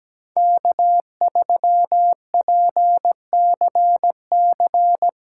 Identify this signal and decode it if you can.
Morse code, heard as a keyed tone, about 17 wpm, 700 Hz. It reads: K3PCC
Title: USA Callsign DE Callsign - 5 Characters - 17wpm